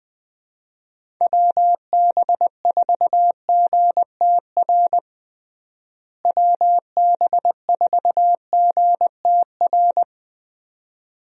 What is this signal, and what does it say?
Morse code, heard as a keyed tone, about 20 wpm, 700 Hz. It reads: WB4GTR WB4GTR
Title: USA Call Signs - 6 Characters - Repeated 3x - 20wpm